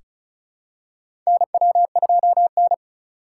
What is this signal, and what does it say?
Morse code, heard as a keyed tone, about 36 wpm, 700 Hz. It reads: NW2N
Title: USA Call Signs - 4 Characters - 3x Character Spacing - 35wpm